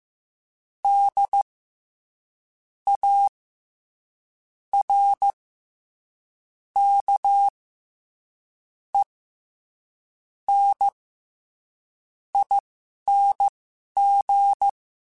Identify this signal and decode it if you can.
Morse code, heard as a keyed tone, about 15 wpm, spaced as for 11 wpm, 800 Hz. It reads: D A R K E N ING
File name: Special INGs-mp3s / Special INGs-mp3s / SlowToFaster ing @10